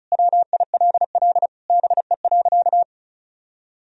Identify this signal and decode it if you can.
Morse code, heard as a keyed tone, about 36 wpm, 700 Hz. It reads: WILL BE.
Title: Sentences from Top 1000 Words - Review Part 4 - 35wpm